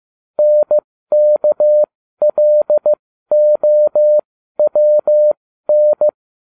Morse code, heard as a keyed tone, about 15 wpm, spaced as for 12 wpm, 600 Hz. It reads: NKLOWN